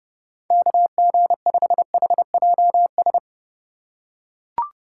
Morse code, heard as a keyed tone, about 29 wpm, 700 Hz. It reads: KG5HJS